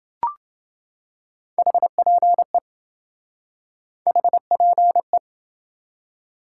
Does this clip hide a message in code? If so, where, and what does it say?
Yes: Morse code from 1.6 to 5.2 s, heard as a keyed tone, about 27 wpm, 700 Hz. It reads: HPE HPE